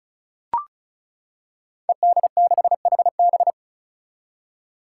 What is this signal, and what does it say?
Morse code, heard as a keyed tone, about 36 wpm, 700 Hz. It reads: ED6HB
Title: Five Challenging Letter-Number Combinations 35wpm